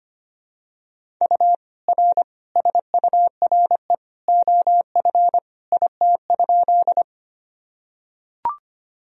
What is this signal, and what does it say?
Morse code, heard as a keyed tone, about 25 wpm, 700 Hz. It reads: U R SURE OF IT?
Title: Sentences from Top 300 Words and Abbreviations 25wpm